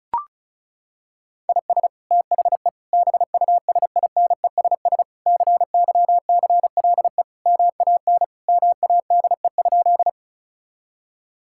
Morse code, heard as a keyed tone, about 36 wpm, 700 Hz. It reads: IS THE BUSINESS CYCLE MAN MADE?
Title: Sentences from Top 1000 Words - Review Part 6 - 35wpm